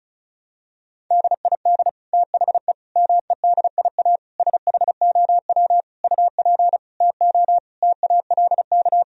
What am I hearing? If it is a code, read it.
Morse code, heard as a keyed tone, about 36 wpm, 700 Hz. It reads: DID THE MEDIA SHOW UP TO TALK
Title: Sentences from Top 1000 Words - Review Part 4 - 35wpm